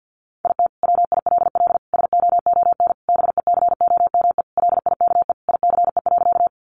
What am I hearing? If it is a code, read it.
Morse code, heard as a keyed tone, about 50 wpm, 700 Hz. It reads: IT WILL SOON BECOME LIKE ICE.
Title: Sentences from Top 1000 Words 50wpm